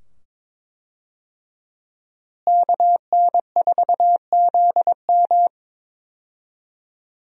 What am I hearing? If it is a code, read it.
Morse code, heard as a keyed tone, about 22 wpm, 700 Hz. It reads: KN4ZM
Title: USA Call Signs - 5 Characters - Repeated 2x - 22wpm